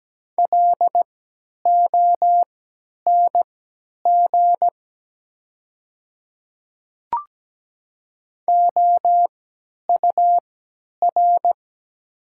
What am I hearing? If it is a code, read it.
Morse code, heard as a keyed tone, about 17 wpm, spaced as for 10 wpm, 700 Hz. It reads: LONG OUR
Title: 3x Character Spacing - Top 200 Words - 17wpm